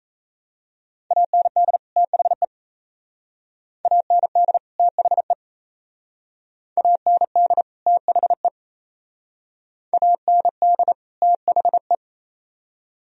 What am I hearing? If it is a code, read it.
Morse code, heard as a keyed tone, about 34 wpm, 690 Hz. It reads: AND THE AND THE AND THE AND THE
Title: Speed-Racing - Sets of 2 Words 28wpm